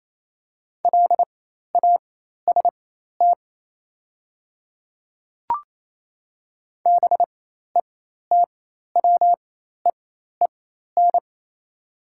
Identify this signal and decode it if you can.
Morse code, heard as a keyed tone, about 28 wpm, spaced as for 13 wpm, 700 Hz. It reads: LAST BETWEEN
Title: 4x Character Spacing - Top 300 Words - 28wpm